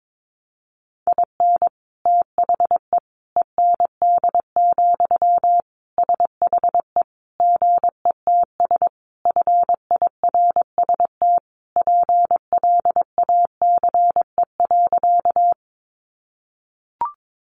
Morse code, heard as a keyed tone, about 22 wpm, 700 Hz. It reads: IN THE END, SHE GETS FIRST PLACE.
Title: Additional Easy Phrases 22wpm